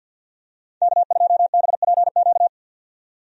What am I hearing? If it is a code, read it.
Morse code, heard as a keyed tone, about 45 wpm, 700 Hz. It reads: K2BPQ
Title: USA Call Signs - 5 Characters - 50wpm